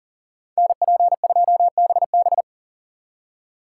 Morse code, heard as a keyed tone, about 52 wpm, 700 Hz. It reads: NP2BB